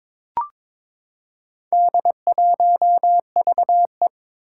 Morse code, heard as a keyed tone, about 22 wpm, 700 Hz. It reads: D1VE
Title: Four Challenging Letter-Number Combinations 22wpm